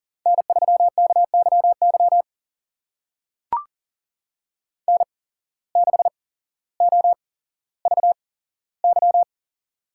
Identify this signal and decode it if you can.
Morse code, heard as a keyed tone, about 39 wpm, spaced as for 17 wpm, 700 Hz. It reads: N3KYY N6OVY
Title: USA Call Signs - 5 Characters - 8x Character Spacing - 40wpm